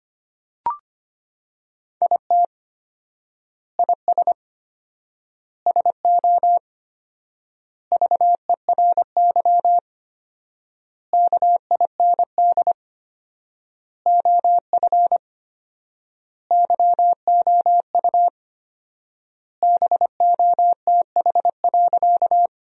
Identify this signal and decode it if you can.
Morse code, heard as a keyed tone, about 25 wpm, 700 Hz. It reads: IT IS SO VERY KIND OF YOU BOTH.
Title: 4x Word-Spacing - Sentences from Top 300 Words - 25wpm